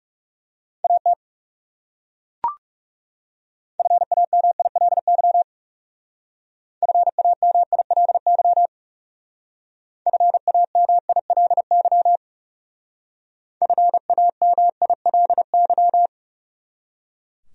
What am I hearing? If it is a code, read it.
Morse code, heard as a keyed tone, about 40 wpm, 700 Hz. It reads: AT FAMILY FAMILY FAMILY FAMILY